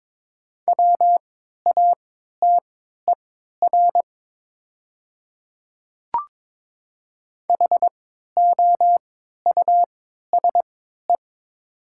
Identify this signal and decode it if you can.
Morse code, heard as a keyed tone, about 23 wpm, spaced as for 13 wpm, 700 Hz. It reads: WATER HOUSE
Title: 3x Character Spacing - Top 300 Words - 22wpm